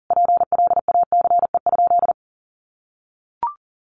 Morse code, heard as a keyed tone, about 40 wpm, 700 Hz. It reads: PLACE?
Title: Sentences from Top 200 Words 40wpm